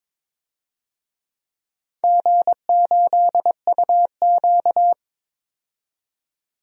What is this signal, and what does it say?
Morse code, heard as a keyed tone, about 22 wpm, 700 Hz. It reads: G8UQ
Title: DXCC-100 Call Signs - 4 Characters - 22wpm